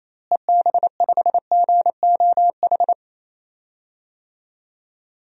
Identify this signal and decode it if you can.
Morse code, heard as a keyed tone, about 28 wpm, 700 Hz. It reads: EB5GOH
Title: DXCC-100 Call Signs - 6 Characters - Repeated 2x - 28wpm